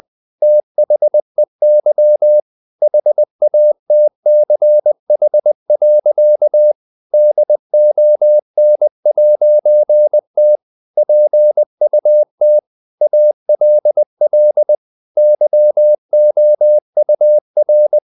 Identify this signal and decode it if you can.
Morse code, heard as a keyed tone, about 20 wpm, 600 Hz. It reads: THEY HATCH. DON'T PUT ALL YOUR
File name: CW / Practice / EnglishSayings20